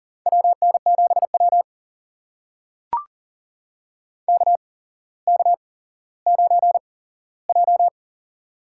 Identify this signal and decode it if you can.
Morse code, heard as a keyed tone, about 39 wpm, spaced as for 17 wpm, 690 Hz. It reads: WN7W KK9J
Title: USA Call Signs - 4 Characters - 8x Character Spacing - 40wpm